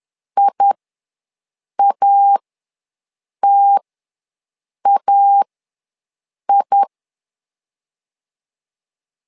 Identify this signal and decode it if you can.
Morse code, heard as a keyed tone, about 11 wpm, spaced as for 6 wpm, 800 Hz. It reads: IATAI
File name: LEZIONI / 2 UHDN / 2 U ws 10wpm 2017